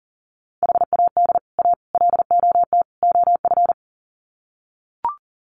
Morse code, heard as a keyed tone, about 40 wpm, 700 Hz. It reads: HAD A LOT OF
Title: Sets of 4 Words 40wpm